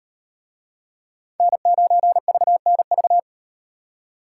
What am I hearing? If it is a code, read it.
Morse code, heard as a keyed tone, about 38 wpm, 700 Hz. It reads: N9VNV